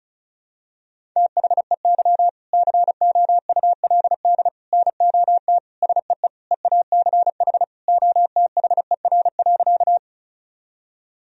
Morse code, heard as a keyed tone, about 36 wpm, 700 Hz. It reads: THEY COULD NOT SEE EACH OTHER.
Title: Sentences from Top 1000 Words - Review Part 1 - 35wpm